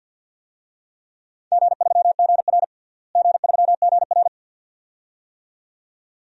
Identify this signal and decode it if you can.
Morse code, heard as a keyed tone, about 50 wpm, 700 Hz. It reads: G3GR G3GR
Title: DXCC-100 Call Signs - 4 Characters - Repeated 2x - 50wpm